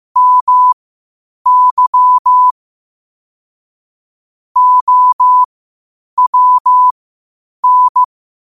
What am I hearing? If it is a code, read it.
Morse code, heard as a keyed tone, about 15 wpm, spaced as for 8 wpm, 1000 Hz. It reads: MY OWN
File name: qotdfiles / 8wpm